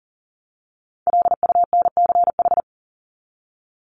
Morse code, heard as a keyed tone, about 40 wpm, 700 Hz. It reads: LUNCH
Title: Top 1751-2000 Words 40wpm